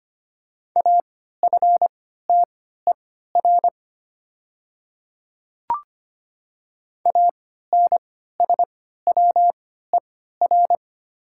Morse code, heard as a keyed tone, about 25 wpm, spaced as for 14 wpm, 700 Hz. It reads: AFTER ANSWER
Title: 3x Character Spacing - Top 500 Words - 25wpm